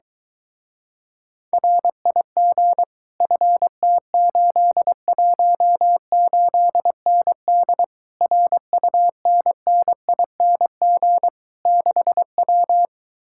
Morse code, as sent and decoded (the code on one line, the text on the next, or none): .-. .. --. / ..-. - ---.. .---- ---.. -. -.. / .-. ..- -. -. .. -. --. / -.... .--
RIG FT818ND RUNNING 6W